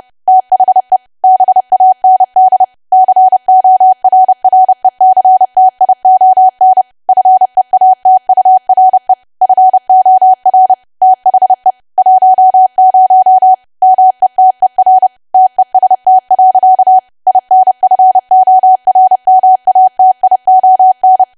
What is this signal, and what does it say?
Morse code, heard as a keyed tone, about 30 wpm, 750 Hz. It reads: THE BAND CORRECTION FEATURE FOR THE 10 METER TEST. INFORMATION
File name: cwa / W1AW-Level3 / 30WPM